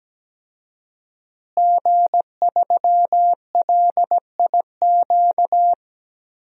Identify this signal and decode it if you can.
Morse code, heard as a keyed tone, about 17 wpm, 700 Hz. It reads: G3LIQ
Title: DXCC-100 Call Signs - 5 Characters - Repeated 3x - 17wpm